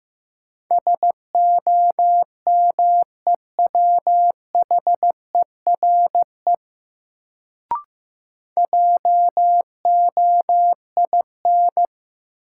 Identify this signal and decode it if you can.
Morse code, heard as a keyed tone, about 15 wpm, 700 Hz. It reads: SOMEWHERE JOIN
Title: Sentences from Top 1400 Words - New Word Drill Alt - 15wpm